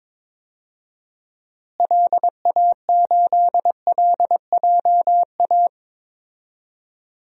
Morse code, heard as a keyed tone, about 22 wpm, 700 Hz. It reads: LA8LJA
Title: DXCC-100 Call Signs - 6 Characters - Repeated 2x - 22wpm